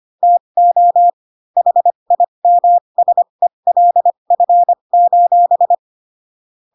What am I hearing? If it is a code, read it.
Morse code, heard as a keyed tone, about 25 wpm, spaced as for 22 wpm, 700 Hz. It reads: TO HIMSELF: